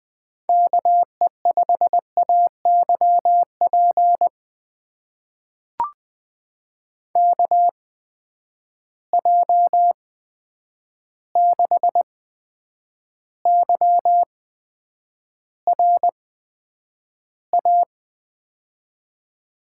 Morse code, heard as a keyed tone, about 20 wpm, spaced as for 9 wpm, 700 Hz. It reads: KE5AYP KJ6YRA
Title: USA Call Signs - 6 Characters - 8x Character Spacing - 20wpm